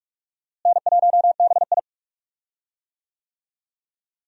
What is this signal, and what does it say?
Morse code, heard as a keyed tone, about 46 wpm, 700 Hz. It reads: N1BI
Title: USA Call Signs - 4 Characters - Repeated 2x - 45wpm